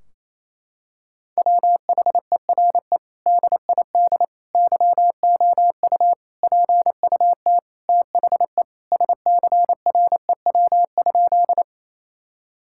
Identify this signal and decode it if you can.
Morse code, heard as a keyed tone, about 28 wpm, 700 Hz. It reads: WHERE DID YOU PUT THE SCREW?
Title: Sentences from Top 1600 Words 28wpm